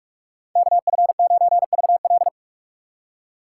Morse code, heard as a keyed tone, about 55 wpm, 700 Hz. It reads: KF9VL
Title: Speed Racing - USA 2x2 Call Signs 45wpm